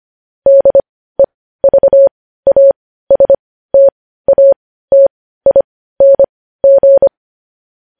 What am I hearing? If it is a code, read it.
Morse code, heard as a keyed tone, about 25 wpm, spaced as for 15 wpm, 550 Hz. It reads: DEVASTATING
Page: News Headlines in Morse Code at 15 WPM